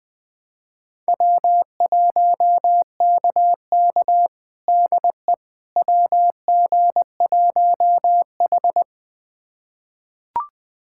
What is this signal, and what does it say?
Morse code, heard as a keyed tone, about 20 wpm, 700 Hz. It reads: W1KK DE WG1H
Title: USA Callsign DE Callsign - 4 Characters - 20wpm